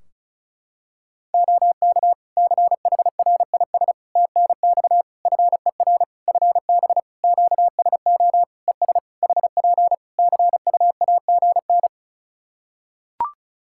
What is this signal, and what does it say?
Morse code, heard as a keyed tone, about 34 wpm, 700 Hz. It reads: OK CHRIS TNX FER FB QSO ES HP CUAGN
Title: Practice QSO - Exchange 4 - Ending - 35wpm